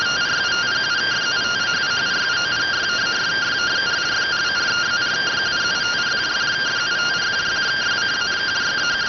Two-tone teleprinter signal, about 170 Hz shift.